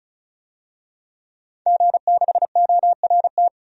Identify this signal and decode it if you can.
Morse code, heard as a keyed tone, about 36 wpm, 700 Hz. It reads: G6ORT
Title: DXCC-100 Call Signs - 5 Characters - 35wpm